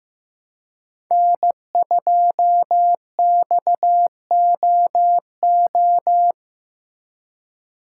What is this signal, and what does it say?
Morse code, heard as a keyed tone, about 15 wpm, 700 Hz. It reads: N2XOO
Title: USA Call Signs - 5 Characters - 2x Character Spacing - 15wpm